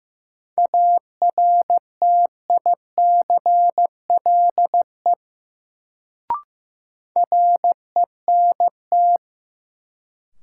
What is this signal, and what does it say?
Morse code, heard as a keyed tone, about 15 wpm, 700 Hz. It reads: ARTICLE RENT
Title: Sentences from Top 1000 Words - Review Part 5 - New Word Drill - 15wpm